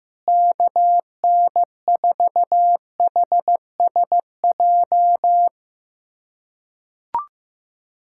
Morse code, heard as a keyed tone, about 15 wpm, 700 Hz. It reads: KN4HSJ
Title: USA Callsign DE Callsign 15wpm